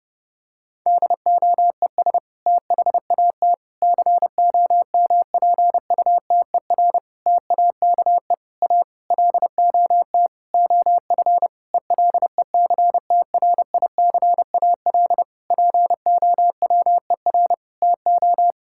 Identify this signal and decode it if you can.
Morse code, heard as a keyed tone, about 32 wpm, 700 Hz. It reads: DOES THAT COMPUTER TAKE A LOT OF ELECTRICAL POWER TO
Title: Sentences from Top 1000 Words - Review Part 6 - 30wpm